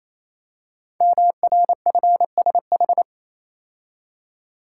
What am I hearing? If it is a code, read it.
Morse code, heard as a keyed tone, about 27 wpm, 700 Hz. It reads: MRFSH
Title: Code Groups - 5 Letters - 28wpm